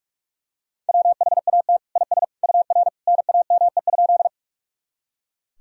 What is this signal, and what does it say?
Morse code, heard as a keyed tone, about 41 wpm, 700 Hz. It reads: WHAT IS UR NAME?
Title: Sentences from Top 200 Words and Abbreviations 45wpm